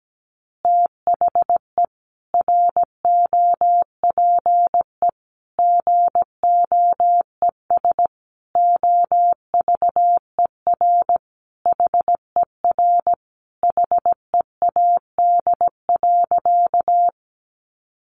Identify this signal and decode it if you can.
Morse code, heard as a keyed tone, about 17 wpm, 700 Hz. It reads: THE ROPE GOES OVER HER HEAD.
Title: Easy Phrases 17wpm